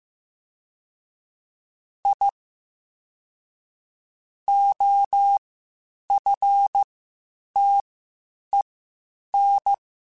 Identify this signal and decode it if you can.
Morse code, heard as a keyed tone, about 15 wpm, spaced as for 8 wpm, 800 Hz. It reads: I OFTEN